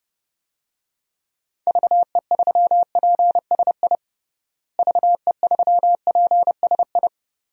Morse code, heard as a keyed tone, about 29 wpm, 700 Hz. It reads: VE3PSI VE3PSI
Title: DXCC-100 Call Signs - 6 Characters - Repeated 2x - 30wpm